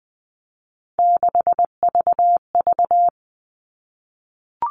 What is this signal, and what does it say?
Morse code, heard as a keyed tone, about 20 wpm, 700 Hz. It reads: 6VV
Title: All Cut Numbers - 3-Digit 20wpm